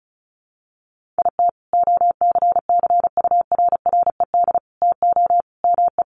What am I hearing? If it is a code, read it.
Morse code, heard as a keyed tone, about 34 wpm, 700 Hz. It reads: IT OCCURRED TO ME
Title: Sets of 5 Words Encore 35wpm